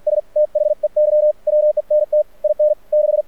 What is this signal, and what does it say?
Morse code, heard as a keyed tone, about 48 wpm, spaced as for 41 wpm, 600 Hz. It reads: STHEY WENT IN 6
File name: archived / CRSnet